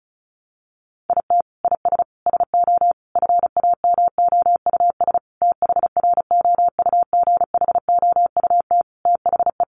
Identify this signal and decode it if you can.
Morse code, heard as a keyed tone, about 34 wpm, 700 Hz. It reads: IT IS SO FAMOUS THROUGHOUT THE
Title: Sentences from Top 1500 Words 35wpm